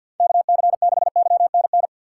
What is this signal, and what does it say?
Morse code, heard as a keyed tone, about 52 wpm, 700 Hz. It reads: KC6YNN
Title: USA Call Signs - 6 Characters - 4x Character Spacing - 50wpm